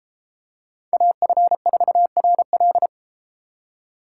Morse code, heard as a keyed tone, about 33 wpm, 700 Hz. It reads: AF4RL